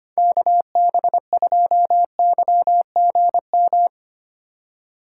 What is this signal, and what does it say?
Morse code, heard as a keyed tone, about 25 wpm, 700 Hz. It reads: KB2YGM
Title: USA Call Signs - 6 Characters - 4x Character Spacing - 25wpm